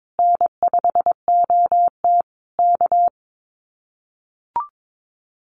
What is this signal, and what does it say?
Morse code, heard as a keyed tone, about 22 wpm, 700 Hz. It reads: N5OT K
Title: Common QSO Phrases Encore 22wpm